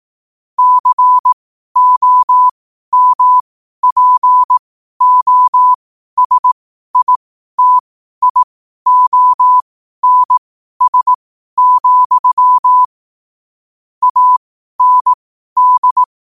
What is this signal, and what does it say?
Morse code, heard as a keyed tone, about 18 wpm, spaced as for 12 wpm, 1000 Hz. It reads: COMPOSITIONS, AND
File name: qotdfiles / 12wpm